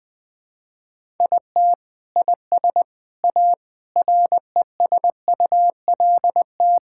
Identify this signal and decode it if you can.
Morse code, heard as a keyed tone, about 20 wpm, 700 Hz. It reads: IT IS A RESULT
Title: Sentences with 4-Word Phrases - 20wpm